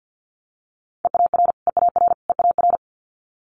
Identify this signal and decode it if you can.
Morse code, heard as a keyed tone, about 50 wpm, 700 Hz. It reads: EAR EAR EAR
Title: Multi-letter Phonemes Repeated 3x 50wpm